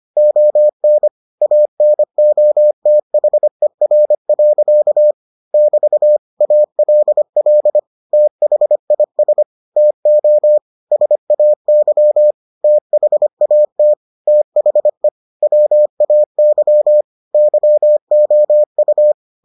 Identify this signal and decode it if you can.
Morse code, heard as a keyed tone, about 25 wpm, 600 Hz. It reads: ON ANOTHER. <BT> ALL THIS TO SAY THAT THE WAY YOU